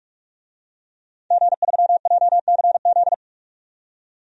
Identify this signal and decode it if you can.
Morse code, heard as a keyed tone, about 46 wpm, 700 Hz. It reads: G3JCZ
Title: DXCC-100 Call Signs - 5 Characters - 45wpm